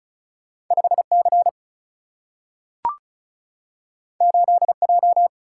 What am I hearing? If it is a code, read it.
Morse code, heard as a keyed tone, about 36 wpm, 700 Hz. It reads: 5 C 8 J
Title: Single-Letter Single-Number 35wpm